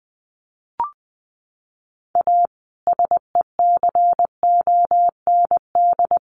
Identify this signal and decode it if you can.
Morse code, heard as a keyed tone, about 20 wpm, 700 Hz. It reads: A SECOND
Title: Sentences from Top 1100 Words 20wpm